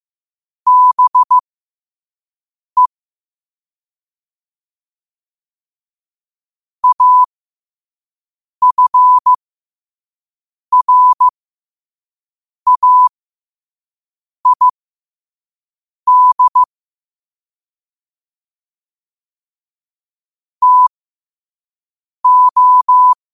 Morse code, heard as a keyed tone, about 15 wpm, spaced as for 5 wpm, 1000 Hz. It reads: BE AFRAID TO
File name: qotdfiles / 5wpm